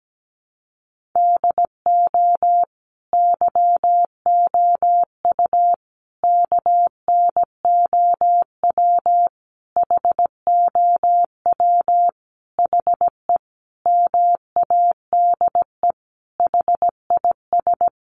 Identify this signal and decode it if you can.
Morse code, heard as a keyed tone, about 17 wpm, 700 Hz. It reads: DO YOU KNOW HOW HE MADE HIS
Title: Sentences from Top 300 Words 17wpm